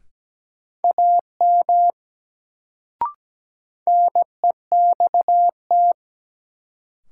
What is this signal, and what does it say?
Morse code, heard as a keyed tone, about 17 wpm, 700 Hz. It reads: AM NEXT
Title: Sets of 2 Words - New Word Drill - 17wpm